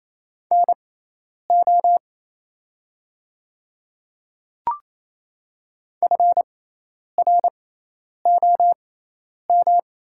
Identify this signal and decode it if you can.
Morse code, heard as a keyed tone, about 28 wpm, spaced as for 10 wpm, 700 Hz. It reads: NO FROM
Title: 6x Character Spacing - Top 100 Words - 28wpm